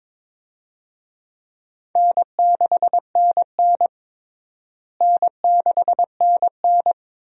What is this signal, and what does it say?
Morse code, heard as a keyed tone, about 22 wpm, 700 Hz. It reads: N6NN N6NN
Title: USA Call Signs - 4 Characters - Repeated 2x - 22wpm